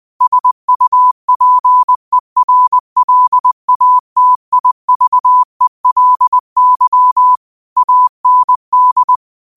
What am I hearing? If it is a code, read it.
Morse code, heard as a keyed tone, about 20 wpm, 1000 Hz. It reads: SUPERLATIVELY AND